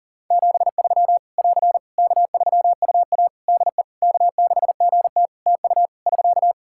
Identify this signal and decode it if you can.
Morse code, heard as a keyed tone, about 40 wpm, 700 Hz. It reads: 73 <AR> K3UA DE K6GT TU <SK>
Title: Practice QSO - Exchange 4 - Ending - 40wpm